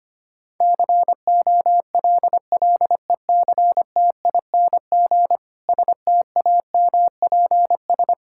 Morse code, heard as a keyed tone, about 25 wpm, 700 Hz. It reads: COLLECTING STAMPS
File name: practice / 25wpm / Difficult Sentences with Hints - 25wpm